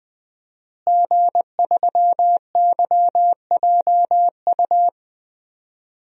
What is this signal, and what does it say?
Morse code, heard as a keyed tone, about 21 wpm, 700 Hz. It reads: G3YJU
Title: DXCC-100 Call Signs - 5 Characters - Repeated 3x - 20wpm